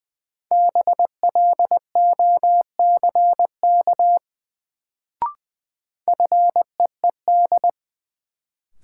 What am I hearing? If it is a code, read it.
Morse code, heard as a keyed tone, about 20 wpm, 700 Hz. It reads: BLOCK FEED
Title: Sentences from Top 1600 Words - New Word Drill - 20wpm